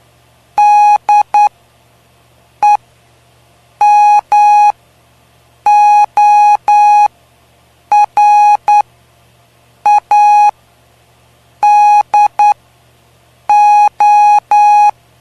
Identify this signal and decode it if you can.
Morse code, heard as a keyed tone, about 9 wpm, spaced as for 6 wpm, 830 Hz. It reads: DEMORADO
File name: cw / py2lcd / 05-PY2LCD-Li��o 5